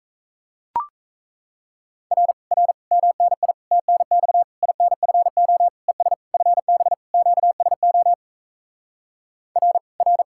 Morse code, heard as a keyed tone, about 37 wpm, 700 Hz. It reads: R R MNI TNX INFO ES FB QSO R R